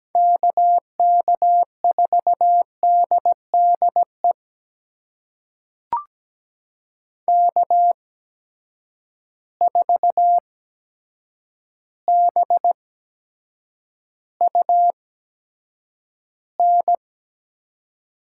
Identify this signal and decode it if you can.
Morse code, heard as a keyed tone, about 17 wpm, spaced as for 8 wpm, 700 Hz. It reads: KK4DDE K4BUN
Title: USA Call Signs - 8x Character Spacing - 17wpm